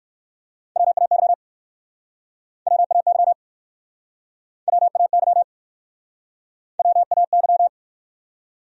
Morse code, heard as a keyed tone, about 59 wpm, 700 Hz. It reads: WAY WAY WAY WAY